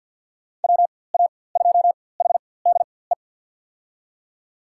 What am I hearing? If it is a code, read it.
Morse code, heard as a keyed tone, about 48 wpm, spaced as for 23 wpm, 700 Hz. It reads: WA2HDE
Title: USA Call Signs - 6 Characters - 4x Character Spacing - 50wpm